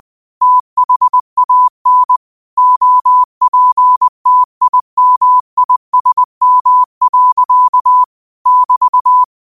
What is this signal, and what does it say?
Morse code, heard as a keyed tone, about 20 wpm, 1000 Hz. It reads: THAN OPTIMISM. <BT>